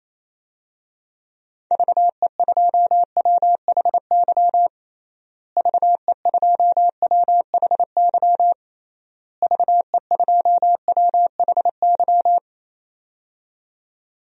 Morse code, heard as a keyed tone, about 27 wpm, 700 Hz. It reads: VE2WHY VE2WHY VE2WHY
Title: DXCC-100 Call Signs - 6 Characters - Repeated 3x - 28wpm